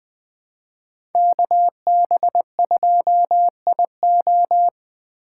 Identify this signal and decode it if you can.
Morse code, heard as a keyed tone, about 20 wpm, 700 Hz. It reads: KB2IO
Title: USA Call Signs - 5 Characters - 3x Character Spacing - 20wpm